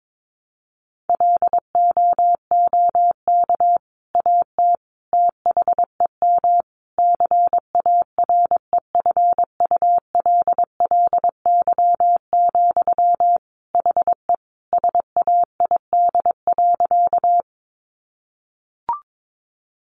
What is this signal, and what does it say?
Morse code, heard as a keyed tone, about 22 wpm, 700 Hz. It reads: LOOK AT THEM CAREFULLY, HE SAID.
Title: Sentences from Top 1300 Words 22wpm